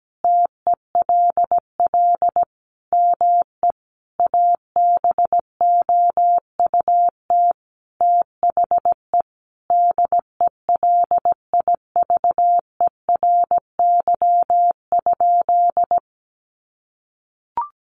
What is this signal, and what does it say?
Morse code, heard as a keyed tone, about 17 wpm, 700 Hz. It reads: TELL ME ABOUT THE DELIVERY?
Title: Sentences from Top 1500 Words 17wpm